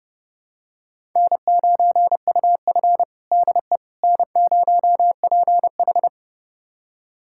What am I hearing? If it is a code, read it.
Morse code, heard as a keyed tone, about 32 wpm, 700 Hz. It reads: N9UF DE N0PH
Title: USA Callsign DE Callsign - 4 Characters - 30wpm